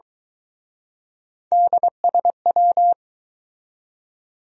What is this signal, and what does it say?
Morse code, heard as a keyed tone, about 24 wpm, 700 Hz. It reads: DSW